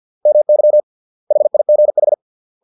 Morse code, heard as a keyed tone, about 50 wpm, 600 Hz. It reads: MY HIGH